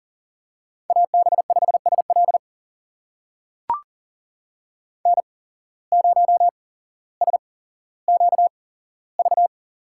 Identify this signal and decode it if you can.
Morse code, heard as a keyed tone, about 40 wpm, spaced as for 17 wpm, 700 Hz. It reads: AB5SL N0SQV
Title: USA Call Signs - 5 Characters - 8x Character Spacing - 40wpm